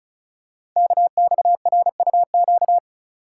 Morse code, heard as a keyed tone, about 36 wpm, 700 Hz. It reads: KXRUQ
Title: Code Groups - 5 Letters - 35wpm